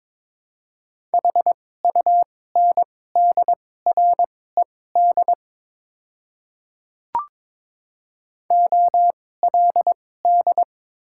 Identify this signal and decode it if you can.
Morse code, heard as a keyed tone, about 22 wpm, spaced as for 16 wpm, 700 Hz. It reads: HUNDRED OLD
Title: 2x Character Spacing - Top 500 Words - 22wpm